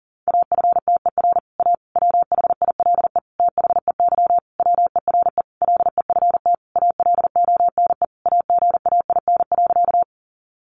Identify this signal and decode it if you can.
Morse code, heard as a keyed tone, about 40 wpm, 700 Hz. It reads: AFTER A WHILE THEY WERE LEFT ALONE AGAIN.
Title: Sentences from Top 1000 Words 40wpm